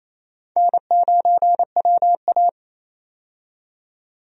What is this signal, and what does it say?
Morse code, heard as a keyed tone, about 28 wpm, 700 Hz. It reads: N9WA
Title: USA Call Signs - 4 Characters - 28wpm